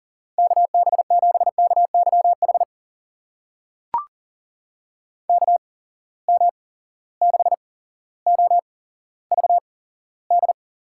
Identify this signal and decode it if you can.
Morse code, heard as a keyed tone, about 40 wpm, spaced as for 17 wpm, 700 Hz. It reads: KB7KYH KM6OVD
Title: USA Call Signs - 6 Characters - 8x Character Spacing - 40wpm